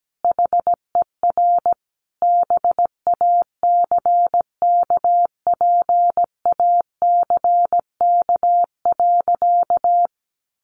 Morse code, heard as a keyed tone, about 17 wpm, 700 Hz. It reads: HER BACKPACK.
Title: Additional Easy Phrases 17wpm